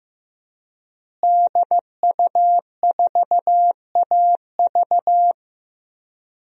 Morse code, heard as a keyed tone, about 15 wpm, 700 Hz. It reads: DU4AV
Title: DXCC-100 Call Signs - 5 Characters - Repeated 2x - 15wpm